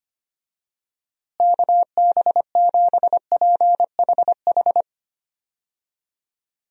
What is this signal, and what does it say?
Morse code, heard as a keyed tone, about 25 wpm, 700 Hz. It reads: KB7PHH
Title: USA Call Signs - 6 Characters - Repeated 3x - 25wpm